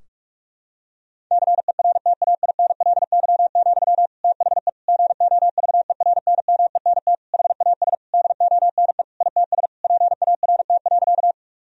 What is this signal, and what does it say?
Morse code, heard as a keyed tone, about 46 wpm, 700 Hz. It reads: CERTAINLY, THE GOVERNMENT HAS DONE ITS PART.
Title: Sentences from Top 1000 Words - Review Part 3 - 45wpm